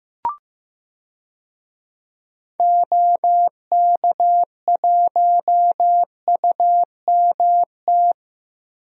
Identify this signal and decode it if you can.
Morse code, heard as a keyed tone, about 15 wpm, 700 Hz. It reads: OK1UMT
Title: DXCC-100 Call Signs - 6 Characters - Repeated 3x - 15wpm